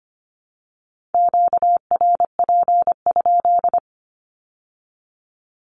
Morse code, heard as a keyed tone, about 24 wpm, 700 Hz. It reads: QRP?
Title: Q-Codes with Spoken Meaning - 25wpm